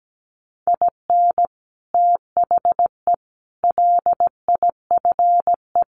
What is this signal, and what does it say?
Morse code, heard as a keyed tone, about 17 wpm, 700 Hz. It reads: IN THE LIFE
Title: Sets of 5 Words Encore2 17wpm